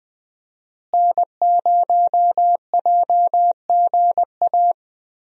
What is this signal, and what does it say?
Morse code, heard as a keyed tone, about 20 wpm, 700 Hz. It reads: N0JGA
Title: USA Call Signs - 5 Characters - Repeated 3x - 20wpm